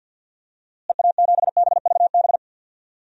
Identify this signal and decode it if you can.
Morse code, heard as a keyed tone, about 50 wpm, 700 Hz. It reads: EA7BVB
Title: DXCC-100 Call Signs - 6 Characters - Repeated 2x - 50wpm